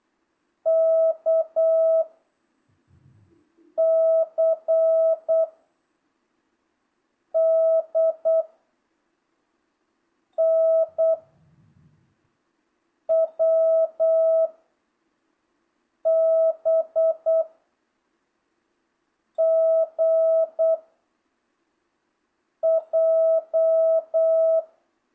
Morse code, heard as a keyed tone, about 8 wpm, 660 Hz. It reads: K C D N W B G J